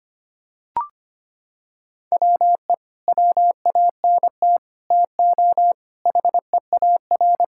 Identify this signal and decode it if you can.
Morse code, heard as a keyed tone, about 25 wpm, 700 Hz. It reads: WE WANT TO HEAR
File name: practice / 25wpm / Sentences with 9-Word Phrases - 25wpm